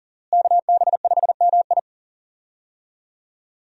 Morse code, heard as a keyed tone, about 40 wpm, 700 Hz. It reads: KB5MI
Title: USA Call Signs - 5 Characters - Repeated 3x - 40wpm